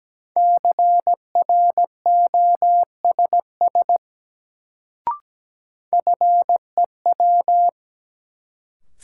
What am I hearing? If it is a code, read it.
Morse code, heard as a keyed tone, about 17 wpm, 700 Hz. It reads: CROSS FEW
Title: Sentences from Top 300 Words - New Word Drill - 17wpm